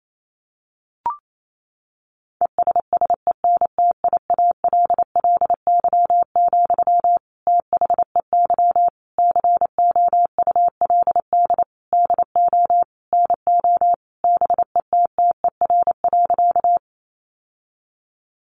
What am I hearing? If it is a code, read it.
Morse code, heard as a keyed tone, about 28 wpm, 700 Hz. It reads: ESSENTIALLY, THEY COULD DO NO BETTER.
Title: Sentences from Top 1300 Words 28wpm